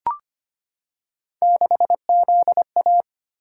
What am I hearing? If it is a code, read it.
Morse code, heard as a keyed tone, about 25 wpm, 700 Hz. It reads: 6ZA
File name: Three Letter-Number Combinations - 25wpm